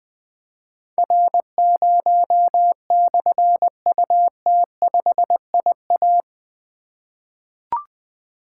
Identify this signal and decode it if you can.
Morse code, heard as a keyed tone, about 20 wpm, 700 Hz. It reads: R0/UT5IA
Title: Difficult Call Signs 20wpm